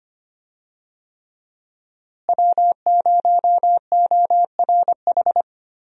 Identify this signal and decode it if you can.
Morse code, heard as a keyed tone, about 25 wpm, 700 Hz. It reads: W0ORH
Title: USA Call Signs - 5 Characters - Repeated 3x - 25wpm